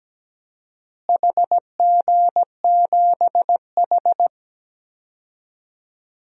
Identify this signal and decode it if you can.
Morse code, heard as a keyed tone, about 17 wpm, 700 Hz. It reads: HG7H
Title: DXCC-100 Call Signs - 4 Characters - Repeated 3x - 17wpm